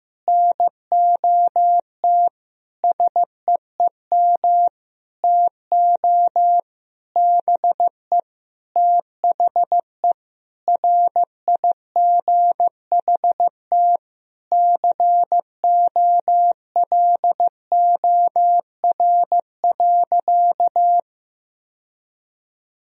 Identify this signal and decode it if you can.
Morse code, heard as a keyed tone, about 15 wpm, 700 Hz. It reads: NOT SEEM TO BE THE RIGHT COLOR.
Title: Sentences from Top 1000 Words - Review Part 2 - 15wpm